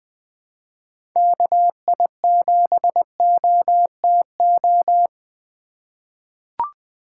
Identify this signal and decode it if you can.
Morse code, heard as a keyed tone, about 20 wpm, 700 Hz. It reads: KI7OTO
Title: USA Call Signs - 6 Characters - 3x Character Spacing - 20wpm